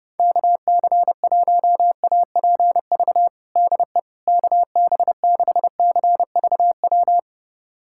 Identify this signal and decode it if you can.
Morse code, heard as a keyed tone, about 32 wpm, 700 Hz. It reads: KC1APV DE KB6CVW
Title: USA Callsign DE Callsign 30wpm